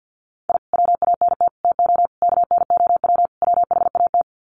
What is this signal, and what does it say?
Morse code, heard as a keyed tone, about 50 wpm, 700 Hz. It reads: I WANT TO KNOW WHAT
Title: Sets of 5 Words Encore2 50wpm